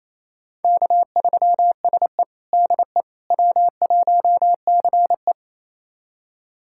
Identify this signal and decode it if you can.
Morse code, heard as a keyed tone, about 28 wpm, 700 Hz. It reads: K3SE DE W1CE
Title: USA Callsign DE Callsign - 4 Characters - 28wpm